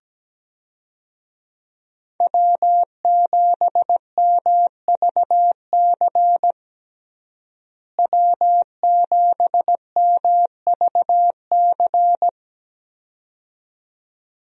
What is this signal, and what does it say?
Morse code, heard as a keyed tone, about 17 wpm, 700 Hz. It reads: W7MVC W7MVC
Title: USA Call Signs - 5 Characters - Repeated 2x - 17wpm